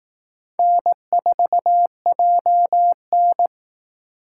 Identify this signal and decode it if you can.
Morse code, heard as a keyed tone, about 19 wpm, 700 Hz. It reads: N4JN